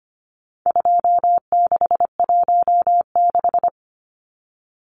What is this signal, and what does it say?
Morse code, heard as a keyed tone, about 25 wpm, 700 Hz. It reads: 2616
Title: Numbers - 4-Digit 25wpm